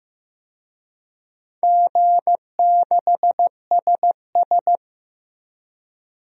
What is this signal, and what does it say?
Morse code, heard as a keyed tone, about 15 wpm, 700 Hz. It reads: G6SS
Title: DXCC-100 Call Signs - 4 Characters - 15wpm